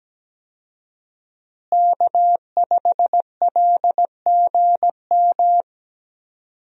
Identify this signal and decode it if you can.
Morse code, heard as a keyed tone, about 17 wpm, 700 Hz. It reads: K5LGM
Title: USA Call Signs - 5 Characters - Repeated 2x - 17wpm